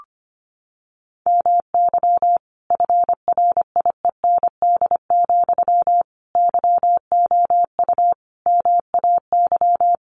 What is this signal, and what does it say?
Morse code, heard as a keyed tone, about 25 wpm, 700 Hz. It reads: MY FRIEND, YOU MAY
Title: Sentences from Top 1300 Words 25wpm